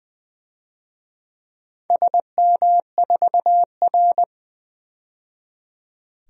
Morse code, heard as a keyed tone, about 20 wpm, 700 Hz. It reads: SM4R
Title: DXCC-100 Call Signs - 4 Characters - 20wpm